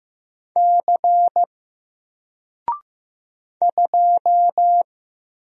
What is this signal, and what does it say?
Morse code, heard as a keyed tone, about 15 wpm, 700 Hz. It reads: C 2
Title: Single Letter-Number Alt 15wpm